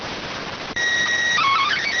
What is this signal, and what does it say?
SSTV (slow-scan television), PD 290.